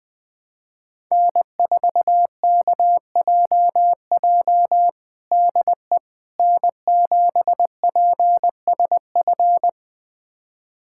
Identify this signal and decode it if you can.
Morse code, heard as a keyed tone, about 21 wpm, 700 Hz. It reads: N4KJJ DE N7PSF
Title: USA Callsign DE Callsign - 5 Characters - 20wpm